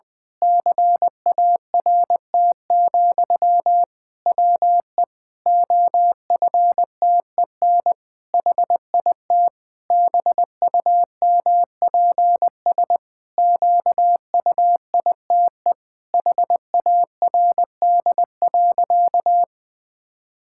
Morse code, heard as a keyed tone, about 21 wpm, 700 Hz. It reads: CART, WE OFTEN HIT BUMPS QUITE HARD.
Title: Pangram Challenge v2 - 20wpm